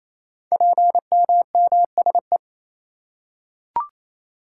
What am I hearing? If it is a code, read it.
Morse code, heard as a keyed tone, about 28 wpm, 700 Hz. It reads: PMMSE
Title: Code Groups - 5 Letters - 28wpm